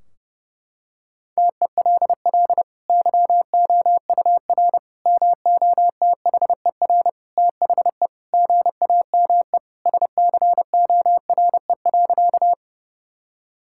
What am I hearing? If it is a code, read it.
Morse code, heard as a keyed tone, about 30 wpm, 700 Hz. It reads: TELL YOUR MOTHER THE GAME SCORE.
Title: Sentences from Top 1600 Words 30wpm